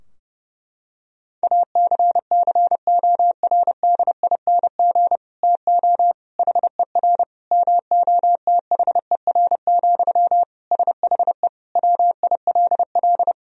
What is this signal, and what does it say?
Morse code, heard as a keyed tone, about 30 wpm, 700 Hz. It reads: ACCORDING TO HER MOTHER, SHE WILL
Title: Sentences from Top 1000 Words - Review Part 4 - 30wpm